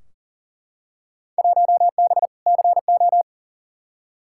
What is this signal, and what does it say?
Morse code, heard as a keyed tone, about 40 wpm, 700 Hz. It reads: 1B CO
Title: Field Day Exchanges - 40wpm